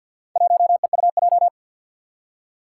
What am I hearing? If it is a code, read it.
Morse code, heard as a keyed tone, about 50 wpm, 700 Hz. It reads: 1EUJ
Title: Four Challenging Letter-Number Combinations 50wpm